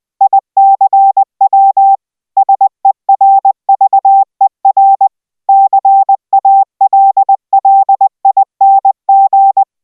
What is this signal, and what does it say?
Morse code, heard as a keyed tone, about 20 wpm, 780 Hz. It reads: ICW SERVER CALLING